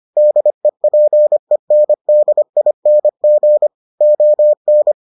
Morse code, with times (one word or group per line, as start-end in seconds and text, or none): 0.2-3.7 DEPENDING
4.0-4.9 ON